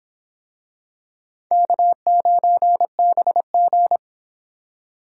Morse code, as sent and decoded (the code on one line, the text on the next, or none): -.- ----. -... --.
K9BG